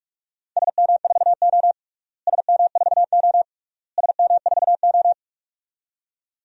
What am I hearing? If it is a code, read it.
Morse code, heard as a keyed tone, about 41 wpm, 700 Hz. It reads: SM4O SM4O SM4O
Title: DXCC-100 Call Signs - 4 Characters - Repeated 3x - 45wpm